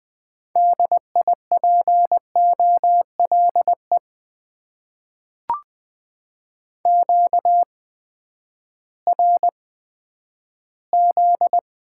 Morse code, heard as a keyed tone, about 20 wpm, spaced as for 11 wpm, 700 Hz. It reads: DIPOLE QRZ
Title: Top 90 QSO Elements - 8x Character Spacing - 20wpm